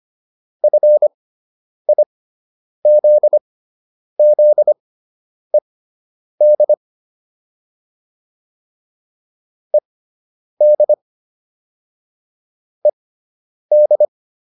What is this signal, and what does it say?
Morse code, heard as a keyed tone, about 25 wpm, spaced as for 9 wpm, 600 Hz. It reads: FIZZED ED ED